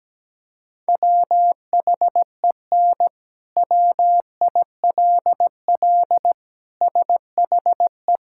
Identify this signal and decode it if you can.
Morse code, heard as a keyed tone, about 17 wpm, 700 Hz. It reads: WHEN WILL SHE
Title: Sentences from Top 1000 Words - Review Part 5 - 17wpm